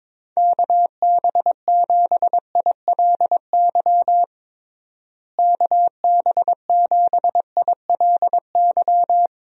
Morse code, heard as a keyed tone, about 22 wpm, 700 Hz. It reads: KB7ILY KB7ILY
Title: USA Call Signs - 6 Characters - Repeated 3x - 22wpm